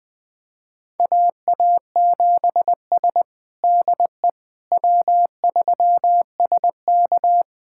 Morse code, heard as a keyed tone, about 20 wpm, 700 Hz. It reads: AA7S DE W3SK
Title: USA Callsign DE Callsign - 4 Characters - 20wpm